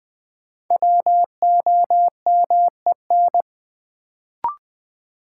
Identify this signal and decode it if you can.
Morse code, heard as a keyed tone, about 20 wpm, 700 Hz. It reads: WOMEN
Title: Sets of 3 Words - New Word Drill - 20wpm